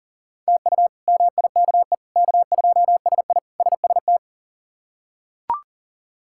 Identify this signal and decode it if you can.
Morse code, heard as a keyed tone, about 40 wpm, 700 Hz. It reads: TU MIKE K2SI SST
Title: K1USN SST QSO Practice - 40wpm